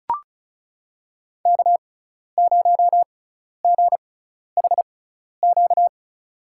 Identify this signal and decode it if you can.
Morse code, heard as a keyed tone, about 36 wpm, spaced as for 12 wpm, 700 Hz. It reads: K0GHQ
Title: USA Call Signs - 5 Characters - 6x Character Spacing - 35wpm